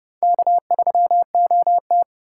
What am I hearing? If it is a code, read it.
Morse code, heard as a keyed tone, about 30 wpm, 700 Hz. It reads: K3OT
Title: USA Call Signs - 2x Character Spacing - 30wpm